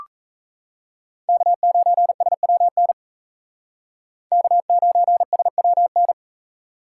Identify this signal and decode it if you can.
Morse code, heard as a keyed tone, about 36 wpm, 700 Hz. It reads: K9SWN K9SWN